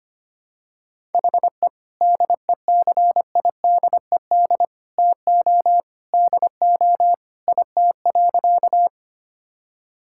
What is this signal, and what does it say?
Morse code, heard as a keyed tone, about 25 wpm, 700 Hz. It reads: HE DECIDED TO DO IT.
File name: practice / 25wpm / Sentences with 3-Word Phrases - 25wpm